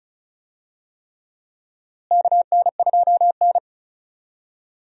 Morse code, heard as a keyed tone, about 36 wpm, 700 Hz. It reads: KN2N
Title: USA Call Signs - 4 Characters - Repeated 3x - 35wpm